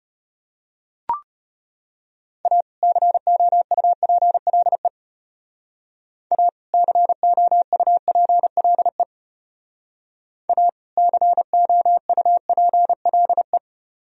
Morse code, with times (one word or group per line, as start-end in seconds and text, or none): 2.5-2.6 A
2.8-4.9 COUPLE
6.3-6.5 A
6.7-9.0 COUPLE
10.5-10.7 A
11.0-13.6 COUPLE